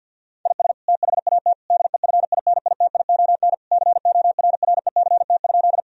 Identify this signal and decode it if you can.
Morse code, heard as a keyed tone, about 50 wpm, 700 Hz. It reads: IS THAT DEFINITION CORRECT?
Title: Sentences from Top 1000 Words - Review Part 4 - 50wpm